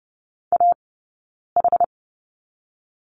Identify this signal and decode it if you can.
Morse code, heard as a keyed tone, about 30 wpm, 700 Hz. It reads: A H
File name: Lesson 18d - Letter m - Two Characters 30wpm